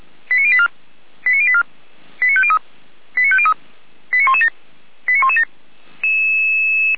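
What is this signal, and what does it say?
Five-tone selective call (ZVEI), heard as quick runs of five tones, tones of about 70 ms. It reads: 89084 89084 89503 89503 89117 89117
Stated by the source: ZVEI five-tone selective call received by SDR.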